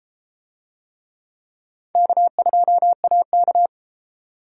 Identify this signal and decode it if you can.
Morse code, heard as a keyed tone, about 32 wpm, 700 Hz. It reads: K2AK